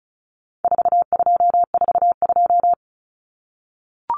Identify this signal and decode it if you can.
Morse code, heard as a keyed tone, about 34 wpm, 700 Hz. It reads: 4242
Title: Numbers - 4-Digit 35wpm